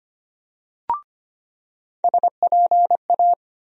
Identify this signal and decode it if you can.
Morse code, heard as a keyed tone, about 25 wpm, 700 Hz. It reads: SPA
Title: USA Call Sign Postfixes - 3 Letters 25wpm